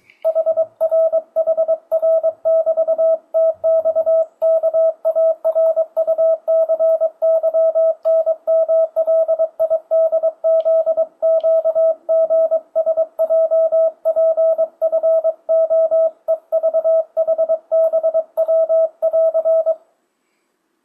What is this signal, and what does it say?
Morse code, heard as a keyed tone, about 22 wpm, 650 Hz. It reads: HRHR<BT>TXKARUCYNMLIDZQGSJPFOEVHBW<AR>
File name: 2025cw / speedup / 23wpm